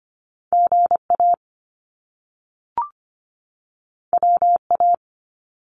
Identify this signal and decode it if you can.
Morse code, heard as a keyed tone, about 25 wpm, 700 Hz. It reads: GA WA
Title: US States 25wpm